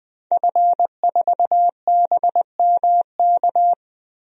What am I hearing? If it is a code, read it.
Morse code, heard as a keyed tone, about 20 wpm, 700 Hz. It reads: F4BMK
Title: DXCC-100 Call Signs - 5 Characters - 20wpm